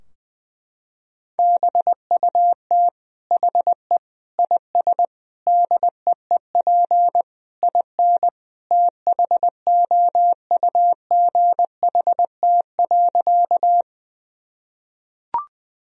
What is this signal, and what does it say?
Morse code, heard as a keyed tone, about 20 wpm, 700 Hz. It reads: BUT HE IS DEEP IN THOUGHT.
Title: Sentences from Top 1000 Words - Review Part 3 - 20wpm